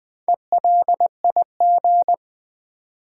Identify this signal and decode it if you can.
Morse code, heard as a keyed tone, about 20 wpm, 700 Hz. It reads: ELIG
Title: Build A Word - 2 Letters At A Time - 8-Letter Words - 20wpm